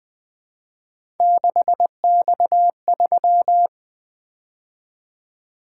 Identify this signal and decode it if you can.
Morse code, heard as a keyed tone, about 20 wpm, 700 Hz. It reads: 6X3
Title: Three Letter-Number Combinations - 20wpm